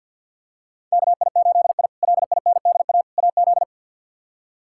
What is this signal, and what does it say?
Morse code, heard as a keyed tone, about 46 wpm, 700 Hz. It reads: KI8I LINDA AZ
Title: K1USN SST QSO Practice - 50wpm